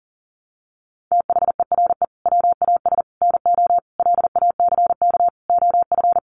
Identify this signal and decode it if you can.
Morse code, heard as a keyed tone, about 40 wpm, 700 Hz. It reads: THERE WAS NO LACK OF
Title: Sentences from Top 1100 Words 40wpm